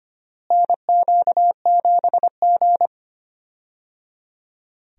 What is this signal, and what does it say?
Morse code, heard as a keyed tone, about 25 wpm, 700 Hz. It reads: NQ7G
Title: USA Call Signs - 4 Characters - Repeated 3x - 25wpm